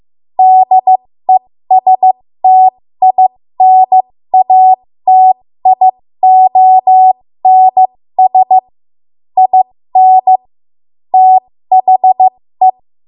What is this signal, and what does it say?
Morse code, heard as a keyed tone, about 15 wpm, spaced as for 13 wpm, 750 Hz. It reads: DESTINATIONS IN THE
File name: Archive / 13 WPM / 13WPM